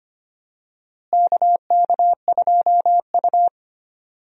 Morse code, heard as a keyed tone, about 25 wpm, 700 Hz. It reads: KK2U
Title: USA Call Signs - 4 Characters - Repeated 2x - 25wpm